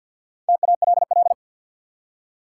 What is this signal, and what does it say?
Morse code, heard as a keyed tone, about 52 wpm, 700 Hz. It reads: TALL
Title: Four Letter Words - 50wpm